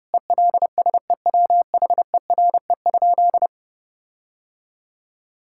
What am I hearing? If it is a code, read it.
Morse code, heard as a keyed tone, about 30 wpm, 700 Hz. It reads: ELSEWHERE?
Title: Sentences from Top 1600 Words 30wpm